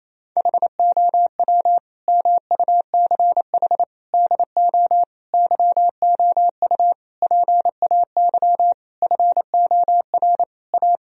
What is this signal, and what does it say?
Morse code, heard as a keyed tone, about 28 wpm, 700 Hz. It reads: HOW MUCH DO YOU PAY FOR A
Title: Sentences from Top 1000 Words - Review Part 3 - 28wpm